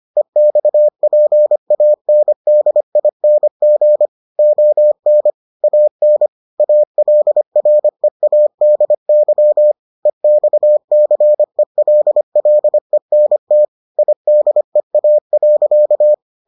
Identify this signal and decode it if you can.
Morse code, heard as a keyed tone, about 25 wpm, 600 Hz. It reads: EXPANDING ON AN ALREADY EXCELLENT IDEA.